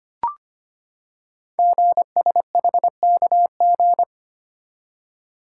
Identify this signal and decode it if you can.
Morse code, heard as a keyed tone, about 25 wpm, 700 Hz. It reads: GSHKG
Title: Code Groups - 5 Letters - 25wpm